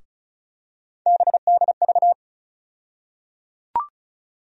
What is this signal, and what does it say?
Morse code, heard as a keyed tone, about 34 wpm, 700 Hz. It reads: BDV
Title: USA Call Sign Postfixes - 3 Letters 35wpm